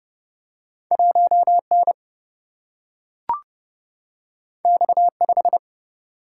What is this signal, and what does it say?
Morse code, heard as a keyed tone, about 30 wpm, 700 Hz. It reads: 1 N X 5
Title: Single-Letter Single-Number 30wpm